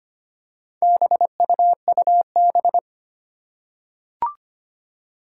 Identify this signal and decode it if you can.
Morse code, heard as a keyed tone, about 25 wpm, 700 Hz. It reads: BUUB
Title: Four Challenging Letter-Number Combinations 25wpm